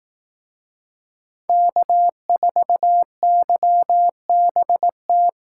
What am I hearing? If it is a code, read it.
Morse code, heard as a keyed tone, about 18 wpm, 700 Hz. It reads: K4YBT
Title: Speed Racing - USA 1x3 Call Signs 15wpm